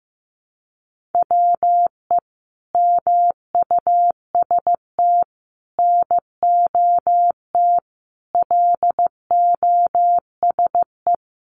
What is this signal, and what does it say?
Morse code, heard as a keyed tone, about 15 wpm, 700 Hz. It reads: WE MUST NOT LOSE
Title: Sentences from Top 1100 Words 15wpm